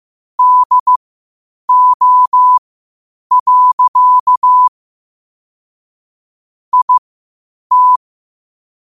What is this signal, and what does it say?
Morse code, heard as a keyed tone, about 15 wpm, spaced as for 8 wpm, 1000 Hz. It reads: DO. IT
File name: qotdfiles / 8wpm